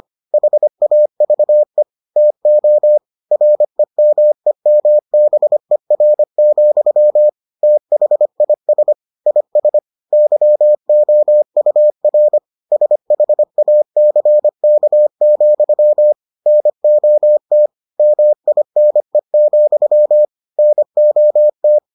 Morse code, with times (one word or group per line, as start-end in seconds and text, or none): 0.3-1.8 HAVE
2.2-3.0 TO
3.3-7.3 REMEMBER,
7.6-8.9 THIS
9.3-9.8 IS
10.1-12.4 YOUR
12.7-16.1 SHACK,
16.5-17.7 NOT
18.0-20.2 MINE,
20.6-21.8 NOT